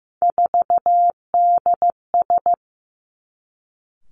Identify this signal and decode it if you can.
Morse code, heard as a keyed tone, about 15 wpm, 700 Hz. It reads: 4DS
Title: Three Challenging Letter-Number Combinations 15wpm